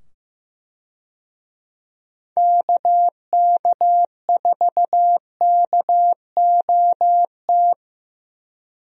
Morse code, heard as a keyed tone, about 15 wpm, 700 Hz. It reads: KK4KOT
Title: USA Call Signs - 6 Characters - 15wpm